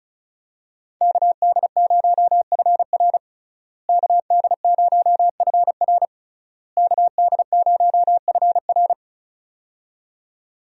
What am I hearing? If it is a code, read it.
Morse code, heard as a keyed tone, about 36 wpm, 700 Hz. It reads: KD0FR KD0FR KD0FR
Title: USA Call Signs - 5 Characters - Repeated 3x - 35wpm